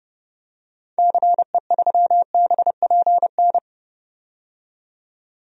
Morse code, heard as a keyed tone, about 30 wpm, 700 Hz. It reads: CE3BPN
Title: DXCC-100 Call Signs - 6 Characters - 30wpm